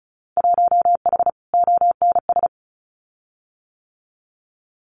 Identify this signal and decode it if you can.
Morse code, heard as a keyed tone, about 34 wpm, 700 Hz. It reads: 1H ONS
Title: Winter Field Day Exchanges 35wpm